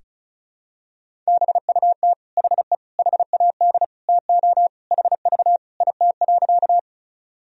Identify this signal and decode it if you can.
Morse code, heard as a keyed tone, about 36 wpm, 700 Hz. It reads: BUT HE HAD TO HV IT.
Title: Sentences from Top 100 Words and Abbreviations 35wpm